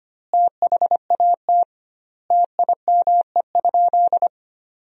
Morse code, heard as a keyed tone, about 25 wpm, 700 Hz. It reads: THAT TIME?
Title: 2x Word-Spacing - Sentences from Top 200 Words - 25wpm